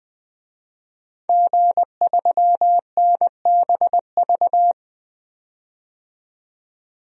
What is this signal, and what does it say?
Morse code, heard as a keyed tone, about 20 wpm, 700 Hz. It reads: G3NBV
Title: DXCC-100 Call Signs - 5 Characters - 20wpm